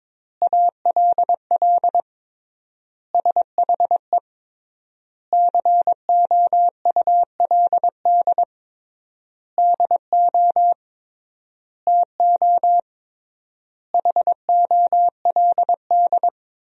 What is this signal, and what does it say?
Morse code, heard as a keyed tone, about 21 wpm, 700 Hz. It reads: ALL SHE COULD DO TO HOLD
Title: 3x Word-Spacing - Sentences from Top 400 Words - 22wpm